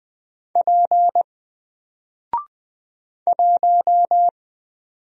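Morse code, heard as a keyed tone, about 20 wpm, 700 Hz. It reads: P 1
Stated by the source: Single letter and number Morse drill at 20 wpm, from N4LCD.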